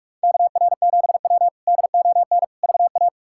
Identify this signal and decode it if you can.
Morse code, heard as a keyed tone, about 41 wpm, 700 Hz. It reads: KR7W DON VA
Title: K1USN SST QSO Practice - 45wpm